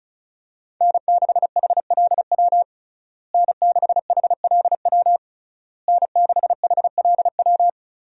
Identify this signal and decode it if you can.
Morse code, heard as a keyed tone, about 36 wpm, 700 Hz. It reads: N6HLW N6HLW N6HLW
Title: USA Call Signs - 5 Characters - Repeated 3x - 35wpm